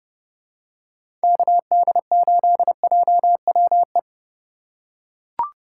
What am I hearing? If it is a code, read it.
Morse code, heard as a keyed tone, about 31 wpm, 700 Hz. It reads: KD8JWE
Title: USA Call Signs - 3x Character Spacing - 30wpm